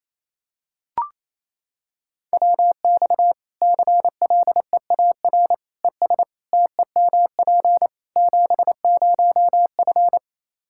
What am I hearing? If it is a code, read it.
Morse code, heard as a keyed tone, about 28 wpm, 700 Hz. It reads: WX CLEAR ES TEMP 70F
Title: Practice QSO - Exchange 2 - Info - 28wpm